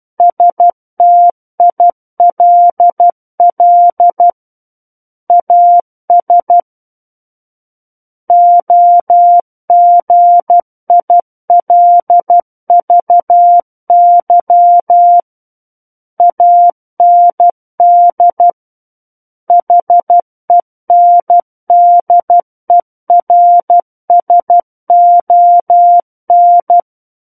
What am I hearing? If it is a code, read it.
Morse code, heard as a keyed tone, about 12 wpm, 700 Hz. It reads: STILL AS OGILVY AND HENDERSON
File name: Book One / 03-Chapter 3 - 12 WPM